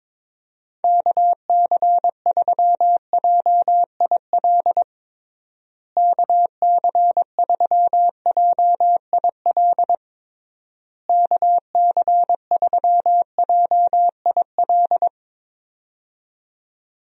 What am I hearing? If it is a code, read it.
Morse code, heard as a keyed tone, about 22 wpm, 700 Hz. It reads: KC3JIL KC3JIL KC3JIL
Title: USA Call Signs - 6 Characters - Repeated 3x - 22wpm